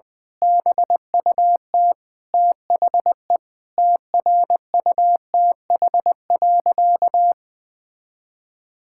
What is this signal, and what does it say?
Morse code, heard as a keyed tone, about 21 wpm, 700 Hz. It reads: BUT THE TRUTH.
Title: Sentences with 9-Word Phrases - Spoken Beginning - 20wpm